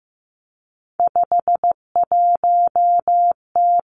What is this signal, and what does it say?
Morse code, heard as a keyed tone, about 15 wpm, 700 Hz. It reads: 51T
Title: Three Challenging Letter-Number Combinations 15wpm